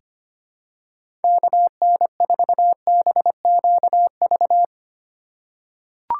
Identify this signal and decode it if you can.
Morse code, heard as a keyed tone, about 25 wpm, 700 Hz. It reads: KN4BQV
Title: USA Call Signs - 6 Characters - 4x Character Spacing - 25wpm